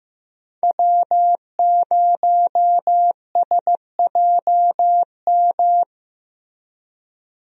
Morse code, heard as a keyed tone, about 16 wpm, 700 Hz. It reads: W0SJM
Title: USA Call Signs - 5 Characters - Repeated 3x - 15wpm